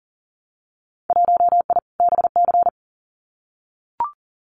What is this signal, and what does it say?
Morse code, heard as a keyed tone, about 39 wpm, 700 Hz. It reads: 1I BC
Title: Winter Field Day Exchanges Encore 40wpm